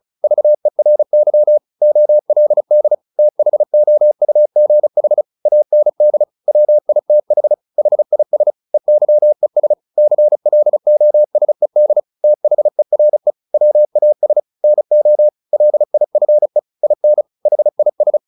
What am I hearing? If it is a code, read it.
Morse code, heard as a keyed tone, about 34 wpm, 600 Hz. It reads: VERY OLD THOUGH AND WITH HIS EYES CLOSED THERE WAS NO LIFE IN HIS
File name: the old man and the sea / the old man and the sea 1 35wpm